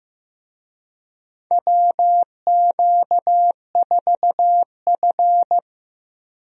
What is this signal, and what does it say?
Morse code, heard as a keyed tone, about 15 wpm, 700 Hz. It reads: WQ4F
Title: USA Call Signs - 4 Characters - Repeated 2x - 15wpm